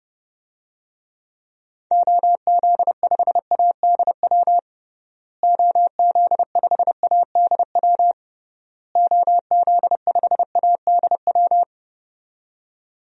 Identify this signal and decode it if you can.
Morse code, heard as a keyed tone, about 30 wpm, 700 Hz. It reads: OZ5ADW OZ5ADW OZ5ADW
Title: DXCC-100 Call Signs - 6 Characters - Repeated 3x - 30wpm